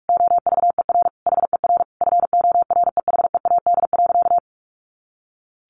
Morse code, heard as a keyed tone, about 44 wpm, 700 Hz. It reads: OVER HER FOREHEAD.
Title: Easy Phrases 45wpm